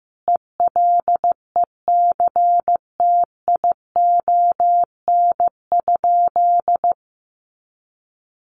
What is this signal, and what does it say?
Morse code, heard as a keyed tone, about 15 wpm, 700 Hz. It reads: ELECTION?
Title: Sentences from Top 1500 Words 15wpm